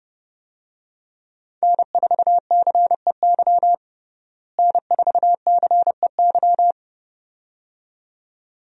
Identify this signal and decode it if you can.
Morse code, heard as a keyed tone, about 30 wpm, 700 Hz. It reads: N4CEY N4CEY
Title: USA Call Signs - 5 Characters - Repeated 2x - 30wpm